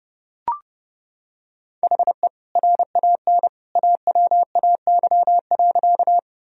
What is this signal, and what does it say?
Morse code, heard as a keyed tone, about 29 wpm, 700 Hz. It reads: HE RAN AWAY.
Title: Sentences from Top 1000 Words - Review Part 3 - 30wpm